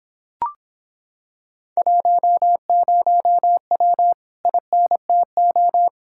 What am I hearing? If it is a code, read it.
Morse code, heard as a keyed tone, about 26 wpm, 700 Hz. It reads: 10W INTO